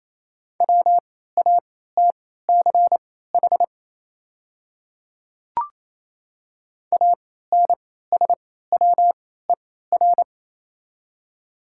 Morse code, heard as a keyed tone, about 28 wpm, spaced as for 16 wpm, 700 Hz. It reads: WATCH ANSWER
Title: 3x Character Spacing - Top 300 Words - 28wpm